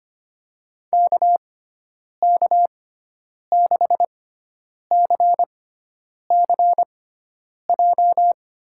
Morse code, heard as a keyed tone, about 25 wpm, spaced as for 9 wpm, 700 Hz. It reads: KK6CCJ
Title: USA Call Signs - 6 Characters - 6x Character Spacing - 25wpm